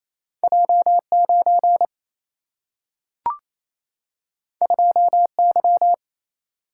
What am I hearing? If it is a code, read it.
Morse code, heard as a keyed tone, about 28 wpm, 700 Hz. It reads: J9 2Y
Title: Single-Letter Single-Number 28wpm